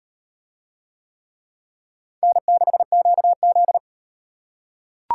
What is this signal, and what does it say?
Morse code, heard as a keyed tone, about 34 wpm, 700 Hz. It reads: N6QZ